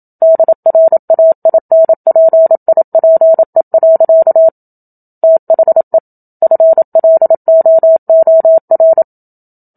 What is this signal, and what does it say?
Morse code, heard as a keyed tone, about 27 wpm, 650 Hz. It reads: DRAINPIPE. THE FLOOR
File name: Book Two / 01-Chapter 1 - 27 WPM